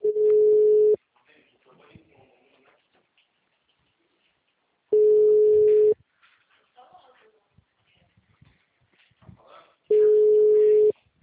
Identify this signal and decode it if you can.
Ringback tone (Europe, 425 Hz), with speech.